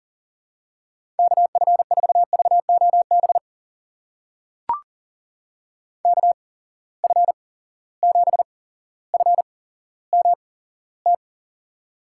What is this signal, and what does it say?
Morse code, heard as a keyed tone, about 40 wpm, spaced as for 17 wpm, 700 Hz. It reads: KF4VOB KF7FMT
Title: USA Call Signs - 6 Characters - 8x Character Spacing - 40wpm